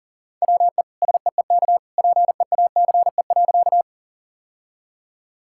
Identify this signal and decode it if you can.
Morse code, heard as a keyed tone, about 40 wpm, 700 Hz. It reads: WE SEEK PEACE.
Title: Sentences from Top 1600 Words 40wpm